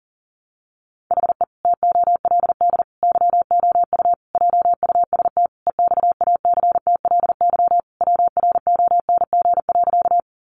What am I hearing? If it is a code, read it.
Morse code, heard as a keyed tone, about 40 wpm, 700 Hz. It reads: HE TOLD YOU JUST EXACTLY WRONG.
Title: Sentences from Top 900 Words 40wpm